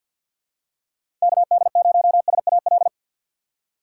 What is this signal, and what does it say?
Morse code, heard as a keyed tone, about 46 wpm, 700 Hz. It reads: KD0SAL
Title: USA Call Signs - 6 Characters - 4x Character Spacing - 50wpm